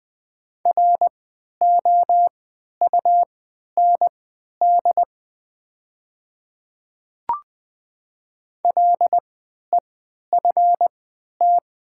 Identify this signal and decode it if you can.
Morse code, heard as a keyed tone, about 20 wpm, spaced as for 11 wpm, 700 Hz. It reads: ROUND LEFT
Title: 3x Character Spacing - Top 300 Words - 20wpm